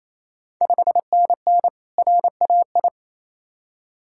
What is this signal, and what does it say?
Morse code, heard as a keyed tone, about 28 wpm, 690 Hz. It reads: 5NN RAI
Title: Ontario QSO Party Exchange 28wpm